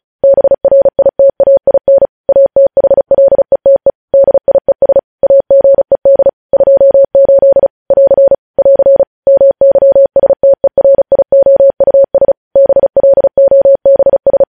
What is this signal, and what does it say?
Morse code, heard as a keyed tone, about 36 wpm, 550 Hz. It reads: BRITAIN ATHLETE DIES AGED 28 <AR> <AR> MYSTERIOUS BLOBS